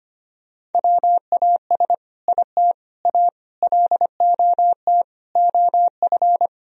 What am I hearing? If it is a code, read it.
Morse code, heard as a keyed tone, about 25 wpm, 700 Hz. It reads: WAS IT A LOT OF
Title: Sentences from Top 1000 Words - Review Part 4 - 25wpm